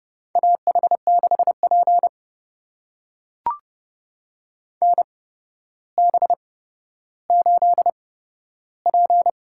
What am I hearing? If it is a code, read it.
Morse code, heard as a keyed tone, about 30 wpm, spaced as for 13 wpm, 700 Hz. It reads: AH6P NB8P
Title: USA Call Signs - 4 Characters - 8x Character Spacing - 30wpm